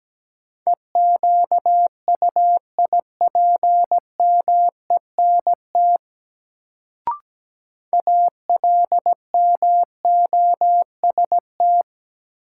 Morse code, heard as a keyed tone, about 17 wpm, 700 Hz. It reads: EQUIPMENT ALMOST
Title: Sentences from Top 700 Words - New Word Drill - 17wpm